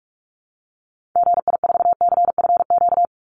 Morse code, heard as a keyed tone, about 44 wpm, 700 Hz. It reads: GI4CFQ
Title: Call Signs Repeated 3x - 45wpm